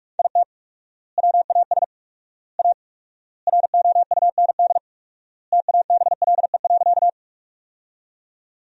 Morse code, heard as a keyed tone, about 41 wpm, 700 Hz. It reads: IT WAS A ROUND TABLE.
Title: 4x Word-Spacing - Sentences from Top 400 Words - 45wpm